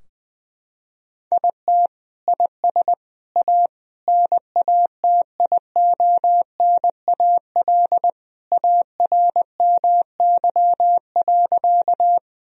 Morse code, heard as a keyed tone, about 20 wpm, 700 Hz. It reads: IT IS A NATIONAL ARMY.
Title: Sentences from Top 1000 Words - Review Part 4 - 20wpm